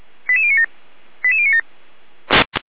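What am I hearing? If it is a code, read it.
Five-tone selective call (ZVEI), heard as quick runs of five tones, tones of about 70 ms. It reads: 88097 88097
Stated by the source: ZVEI five-tone call 89505 received by SDR, from feuerwehr-orlamuende.org fire brigade audio server.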